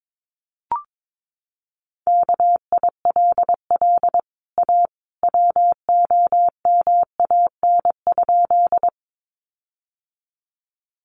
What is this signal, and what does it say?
Morse code, heard as a keyed tone, about 22 wpm, 700 Hz. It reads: KILL A WOMAN?
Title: Sentences from Top 1400 Words 22wpm